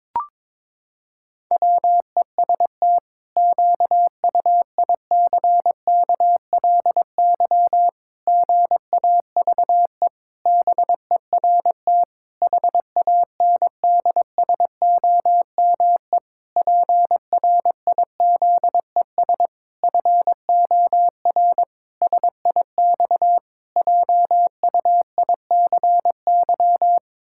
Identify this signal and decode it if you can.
Morse code, heard as a keyed tone, about 22 wpm, 700 Hz. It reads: WEST QUICKLY GAVE BERT HANDSOME PRIZES FOR SIX JUICY
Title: Pangram Challenge v2 - 22wpm